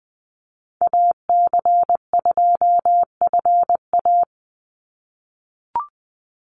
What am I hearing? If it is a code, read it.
Morse code, heard as a keyed tone, about 20 wpm, 700 Hz. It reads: AC2FA
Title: Call Signs - 20wpm